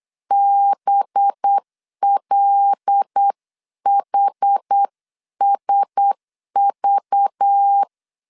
Morse code, heard as a keyed tone, about 8 wpm, 800 Hz. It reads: BLHSV
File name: LEZIONI / 9 2 8wpm 3437